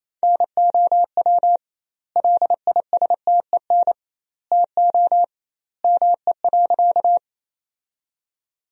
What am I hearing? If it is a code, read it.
Morse code, heard as a keyed tone, about 28 wpm, 700 Hz. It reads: NOW LISTEN TO ME.
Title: 2x Word-Spacing - Sentences from Top 400 Words - 28wpm